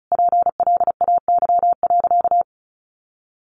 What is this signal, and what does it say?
Morse code, heard as a keyed tone, about 34 wpm, 700 Hz. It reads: PLAY.
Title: Sentences from Top 1500 Words 35wpm